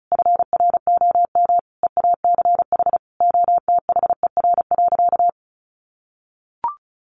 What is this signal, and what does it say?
Morse code, heard as a keyed tone, about 34 wpm, 700 Hz. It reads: FROM EACH OTHER.
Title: Easy Phrases 35wpm